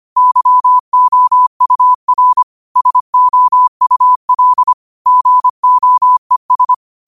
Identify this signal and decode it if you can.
Morse code, heard as a keyed tone, about 25 wpm, 1000 Hz. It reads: YOUR SOUL GOES
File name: qotdfiles / 25wpm